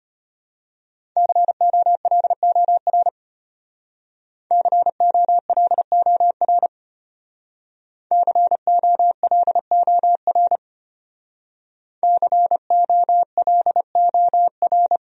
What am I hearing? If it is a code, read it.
Morse code, heard as a keyed tone, about 31 wpm, 700 Hz. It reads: COLOR COLOR COLOR COLOR
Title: Speed-Racing Top 500 Words 25wpm